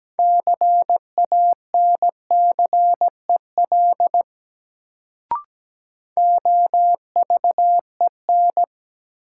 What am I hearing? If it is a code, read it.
Morse code, heard as a keyed tone, about 17 wpm, 700 Hz. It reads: CANCEL OVEN
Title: Sentences from Top 1200 Words - New Word Drill Alt - 17wpm